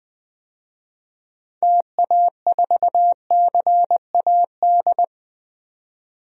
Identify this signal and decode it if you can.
Morse code, heard as a keyed tone, about 20 wpm, 700 Hz. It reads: TA4CAD
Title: DXCC-100 Call Signs - 6 Characters - 20wpm